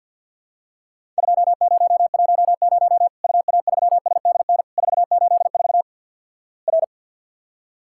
Morse code, heard as a keyed tone, about 50 wpm, 700 Hz. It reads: 2010 UA3SDN 484 R
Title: Worked All Europe DX Contest - QTC Exchange with Repeats - 50wpm